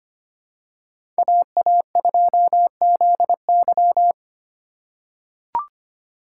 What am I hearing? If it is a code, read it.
Morse code, heard as a keyed tone, about 25 wpm, 700 Hz. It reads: AA2ZY
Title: USA Call Signs - 5 Characters - 4x Character Spacing - 25wpm